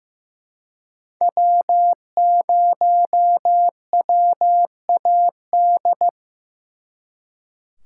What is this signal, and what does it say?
Morse code, heard as a keyed tone, about 18 wpm, 700 Hz. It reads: W0WAD
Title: Speed Racing - USA Call Signs 15wpm